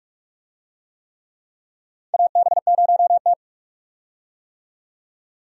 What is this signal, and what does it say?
Morse code, heard as a keyed tone, about 43 wpm, 700 Hz. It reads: AB0T
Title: USA Call Signs - 4 Characters - 45wpm